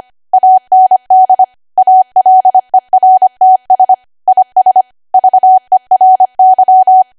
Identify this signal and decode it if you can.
Morse code, heard as a keyed tone, about 25 wpm, 750 Hz. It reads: AND ALERTS IS VERY
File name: Archive / 25 WPM / 25WPM